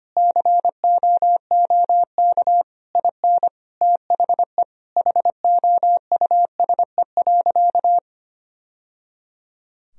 Morse code, heard as a keyed tone, about 25 wpm, 700 Hz. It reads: COOK IN THE HOUSE.
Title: Sentences from Top 1000 Words - Review Part 4 - 25wpm